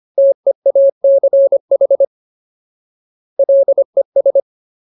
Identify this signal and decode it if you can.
Morse code, heard as a keyed tone, about 25 wpm, 550 Hz. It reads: TEACH LES
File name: Ultimate Word List / 25WPM Ultimate